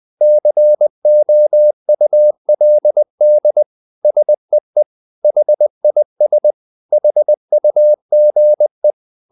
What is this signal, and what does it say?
Morse code, heard as a keyed tone, about 20 wpm, 600 Hz. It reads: COULD SEE HIS HUGE
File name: the old man and the sea part 2 20wpm